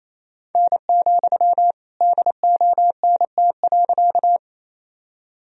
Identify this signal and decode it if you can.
Morse code, heard as a keyed tone, about 28 wpm, 700 Hz. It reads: N, DONT.
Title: Sentences from Top 300 Words and Abbreviations 28wpm